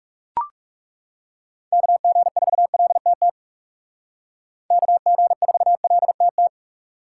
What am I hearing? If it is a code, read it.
Morse code, heard as a keyed tone, about 35 wpm, 700 Hz. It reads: KG4LTT KG4LTT
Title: Speed Racing - USA 2x3 Call Signs 30wpm